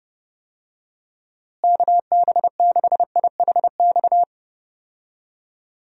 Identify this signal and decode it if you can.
Morse code, heard as a keyed tone, about 30 wpm, 700 Hz. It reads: KB6IHX
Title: USA Call Signs - 6 Characters - Repeated 3x - 30wpm